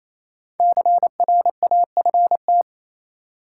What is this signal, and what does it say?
Morse code, heard as a keyed tone, about 28 wpm, 700 Hz. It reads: CRAFT
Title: Sentences from Top 1000 Words - Review Part 6 - New Word Drill - 28wpm